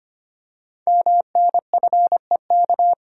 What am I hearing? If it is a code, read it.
Morse code, heard as a keyed tone, about 25 wpm, 700 Hz. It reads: MNFEK
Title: Code Groups - 5 Letters - 25wpm